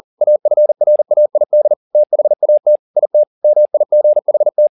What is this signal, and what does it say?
Morse code, heard as a keyed tone, about 40 wpm, 600 Hz. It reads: AFRAID THAT IT MIGHT
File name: the old man and the sea 2 40wpm